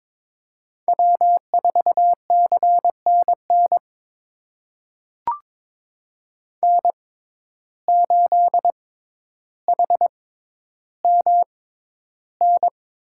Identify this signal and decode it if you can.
Morse code, heard as a keyed tone, about 22 wpm, spaced as for 11 wpm, 700 Hz. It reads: W4CNN N8HMN
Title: USA Call Signs - 5 Characters - 6x Character Spacing - 22wpm